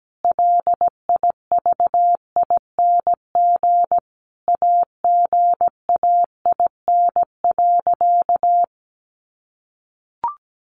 Morse code, heard as a keyed tone, about 17 wpm, 700 Hz. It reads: LIVING AGAIN.
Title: Sentences from Top 1500 Words 17wpm